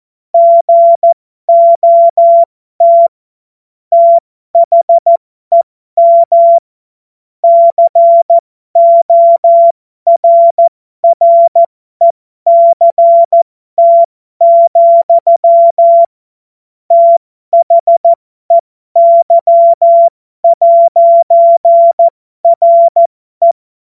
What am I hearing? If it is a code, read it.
Morse code, heard as a keyed tone, about 14 wpm, spaced as for 12 wpm, 680 Hz. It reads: GOT THEM CORRECT, THEY'RE
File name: Morsekurs / Witze